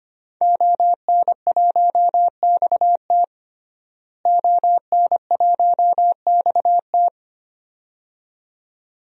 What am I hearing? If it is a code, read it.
Morse code, heard as a keyed tone, about 25 wpm, 700 Hz. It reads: ON1XT ON1XT
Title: DXCC-100 Call Signs - 5 Characters - Repeated 2x - 25wpm